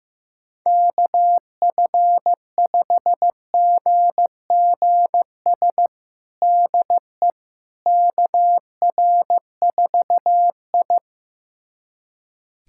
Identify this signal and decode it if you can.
Morse code, heard as a keyed tone, about 15 wpm, 700 Hz. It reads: KF5GGS DE KR4I
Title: USA Callsign DE Callsign 15wpm